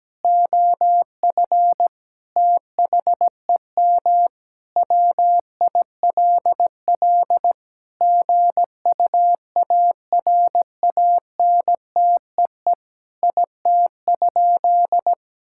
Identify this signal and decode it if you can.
Morse code, heard as a keyed tone, about 17 wpm, 700 Hz. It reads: OF THEM WILL GUARANTEE IT?
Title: Sentences from Top 1600 Words 17wpm